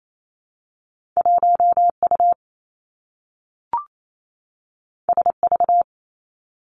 Morse code, heard as a keyed tone, about 28 wpm, 700 Hz. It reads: 1 U S V
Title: Two Challenging Letter-Number Combinations 28wpm